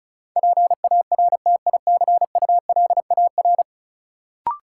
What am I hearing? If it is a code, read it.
Morse code, heard as a keyed tone, about 36 wpm, 700 Hz. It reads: PARTICULAR
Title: Sentences from Top 600 Words - New Word Drill - 35wpm